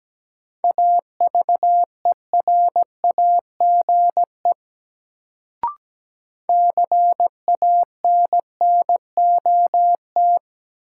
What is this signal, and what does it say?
Morse code, heard as a keyed tone, about 17 wpm, 700 Hz. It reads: AVERAGE CANNOT
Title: Sentences from Top 1000 Words - Review Part 6 - New Word Drill - 17wpm